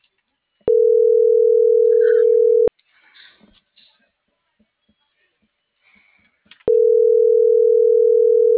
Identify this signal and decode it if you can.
Ringback tone (North America), 440 + 480 Hz.